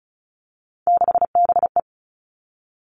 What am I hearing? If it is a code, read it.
Morse code, heard as a keyed tone, about 34 wpm, 700 Hz. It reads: 6BE
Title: All Cut Numbers - 3-Digit 35wpm